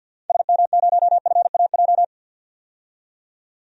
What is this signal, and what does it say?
Morse code, heard as a keyed tone, about 52 wpm, 700 Hz. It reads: SM0FAJ
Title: DXCC-100 Call Signs - 6 Characters - 50wpm